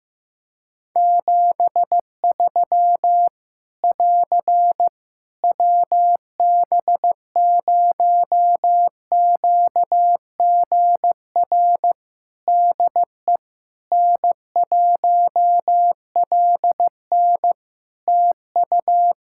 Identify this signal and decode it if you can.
Morse code, heard as a keyed tone, about 15 wpm, 700 Hz. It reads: 73 <AR> WB0QGR DE N1LN TU
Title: Practice QSO - Exchange 4 - Ending - 15wpm